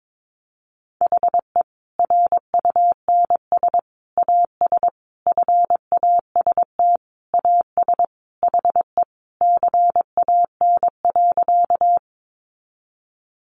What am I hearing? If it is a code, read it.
Morse code, heard as a keyed tone, about 22 wpm, 700 Hz. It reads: HE RUNS AS FAST AS HE CAN.
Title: Additional Easy Phrases 22wpm